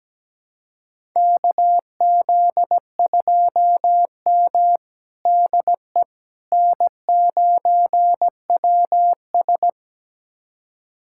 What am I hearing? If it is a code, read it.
Morse code, heard as a keyed tone, about 17 wpm, 700 Hz. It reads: KZ2M DE N9WS
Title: USA Callsign DE Callsign - 4 Characters - 17wpm